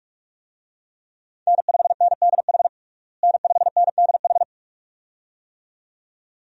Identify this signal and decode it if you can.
Morse code, heard as a keyed tone, about 46 wpm, 700 Hz. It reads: N5NDH N5NDH
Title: USA Call Signs - 5 Characters - Repeated 2x - 45wpm